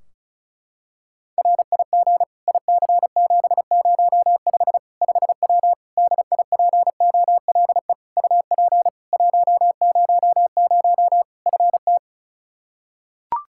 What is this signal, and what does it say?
Morse code, heard as a keyed tone, about 34 wpm, 700 Hz. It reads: RIG IC705 5W DIPOLE UP 100 FT
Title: K1USN SST QSO Practice 35wpm